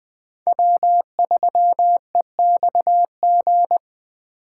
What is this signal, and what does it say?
Morse code, heard as a keyed tone, about 20 wpm, 700 Hz. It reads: W3EXG